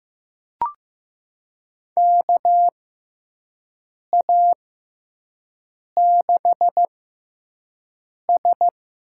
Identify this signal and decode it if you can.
Morse code, heard as a keyed tone, about 15 wpm, spaced as for 5 wpm, 700 Hz. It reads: KA6S
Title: USA Call Signs - 4 Characters - 6x Character Spacing - 15wpm